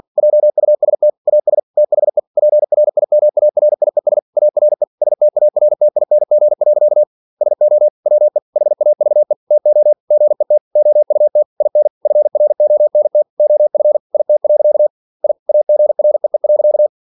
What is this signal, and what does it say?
Morse code, heard as a keyed tone, about 46 wpm, 600 Hz. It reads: JUST AS THE PRIMARIES ARE STARTING. SO WE HAVE TO GET OUT IN FRONT OF IT. I AGREE.
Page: Never 47wpm